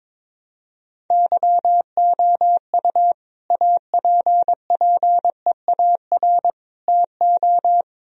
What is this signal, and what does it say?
Morse code, heard as a keyed tone, about 22 wpm, 700 Hz. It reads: YOU APPEAR TO
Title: Sentences from Top 1000 Words - Review Part 2 - 22wpm